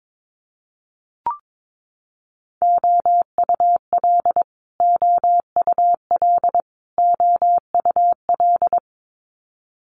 Morse code, heard as a keyed tone, about 22 wpm, 700 Hz. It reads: OUL OUL OUL
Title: Multi-letter Phonemes Repeated 3x 22wpm